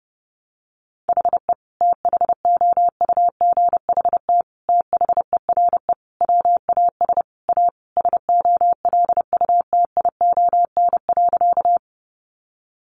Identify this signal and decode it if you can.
Morse code, heard as a keyed tone, about 30 wpm, 700 Hz. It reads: HE THOUGHT THERE WAS A SOLUTION.
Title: Sentences from Top 1300 Words 30wpm